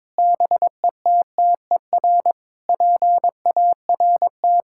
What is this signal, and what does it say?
Morse code, heard as a keyed tone, about 22 wpm, 700 Hz. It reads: BETTER PART
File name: Sentences with 7-Word Phrases - 22wpm